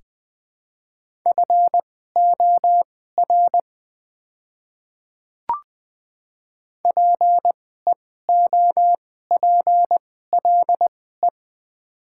Morse code, heard as a keyed tone, about 20 wpm, spaced as for 14 wpm, 700 Hz. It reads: FOR PEOPLE
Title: 2x Character Spacing - Top 100 Words - 20wpm